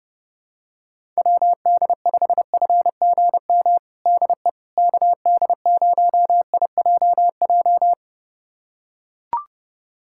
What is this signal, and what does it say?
Morse code, heard as a keyed tone, about 30 wpm, 700 Hz. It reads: WD5FGM DE KD0IJJ
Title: USA Callsign DE Callsign - 6 Characters - 30wpm